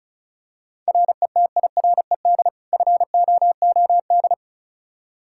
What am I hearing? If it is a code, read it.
Morse code, heard as a keyed tone, about 34 wpm, 700 Hz. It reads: RETIRED FOOD
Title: Practice QSO - Exchange 3 - More Info - 35wpm